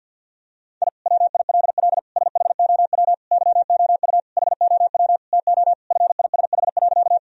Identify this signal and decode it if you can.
Morse code, heard as a keyed tone, about 50 wpm, 700 Hz. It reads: I WILL SHOW YOU HOW TO FISH.
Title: Sentences from Top 1000 Words - Review Part 1 - 50wpm